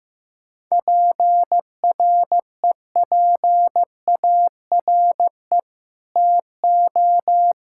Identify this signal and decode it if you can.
Morse code, heard as a keyed tone, about 15 wpm, 700 Hz. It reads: PREPARE TO
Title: Sentences from Top 1000 Words - Review Part 6 - 15wpm